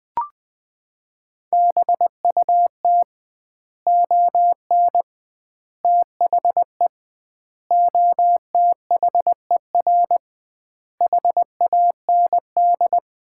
Morse code, heard as a keyed tone, about 21 wpm, 700 Hz. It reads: BUT ON THE OTHER HAND
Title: 2x Word-Spacing - Sets of 5 Words - 20wpm